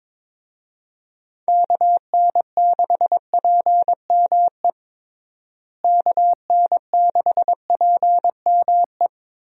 Morse code, heard as a keyed tone, about 22 wpm, 700 Hz. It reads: KN6PME KN6PME
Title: USA Call Signs - 6 Characters - Repeated 2x - 22wpm